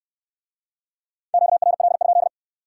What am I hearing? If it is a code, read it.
Morse code, heard as a keyed tone, about 64 wpm, 700 Hz. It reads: QRL?